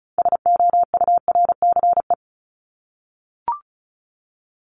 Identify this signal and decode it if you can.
Morse code, heard as a keyed tone, about 34 wpm, 700 Hz. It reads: SOURCE
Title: ES:S Suffix - Top 100 Words with Base Words 35wpm